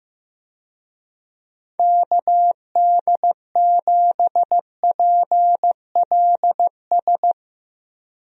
Morse code, heard as a keyed tone, about 15 wpm, 700 Hz. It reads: KD7PLS
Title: USA Call Signs - 6 Characters - Repeated 2x - 15wpm